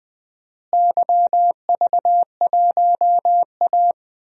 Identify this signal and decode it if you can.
Morse code, heard as a keyed tone, about 20 wpm, 700 Hz. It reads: YV1A
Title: DXCC-100 Call Signs - 4 Characters - Repeated 2x - 20wpm